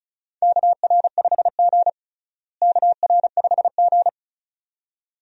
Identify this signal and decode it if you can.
Morse code, heard as a keyed tone, about 34 wpm, 700 Hz. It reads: KR5G KR5G
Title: USA Call Signs - 4 Characters - Repeated 2x - 35wpm